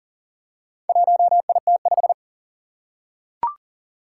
Morse code, heard as a keyed tone, about 40 wpm, 700 Hz. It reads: 1IT5
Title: Four Challenging Letter-Number Combinations 40wpm